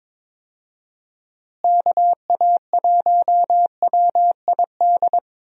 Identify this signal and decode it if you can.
Morse code, heard as a keyed tone, about 22 wpm, 700 Hz. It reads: KA1WID
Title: USA Call Signs - 6 Characters - Repeated 3x - 22wpm